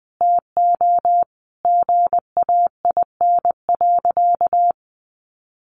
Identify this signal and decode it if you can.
Morse code, heard as a keyed tone, about 20 wpm, 700 Hz. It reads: TO GAIN.
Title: Sentences from Top 1100 Words 20wpm